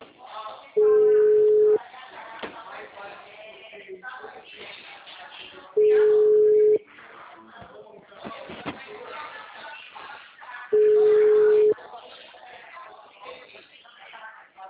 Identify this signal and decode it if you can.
Ringback tone (Europe, 425 Hz), with speech.